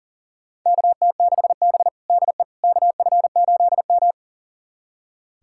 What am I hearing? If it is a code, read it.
Morse code, heard as a keyed tone, about 40 wpm, 700 Hz. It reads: KT6B DE KF8M
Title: USA Callsign DE Callsign - 4 Characters - 40wpm